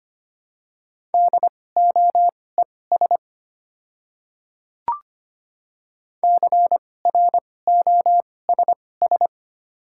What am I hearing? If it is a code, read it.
Morse code, heard as a keyed tone, about 25 wpm, spaced as for 18 wpm, 700 Hz. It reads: DOES CROSS
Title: 2x Character Spacing - Top 500 Words - 25wpm